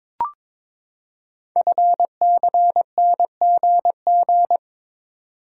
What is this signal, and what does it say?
Morse code, heard as a keyed tone, about 22 wpm, 700 Hz. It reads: FCNGG
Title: Code Groups - 5 Letters - 22wpm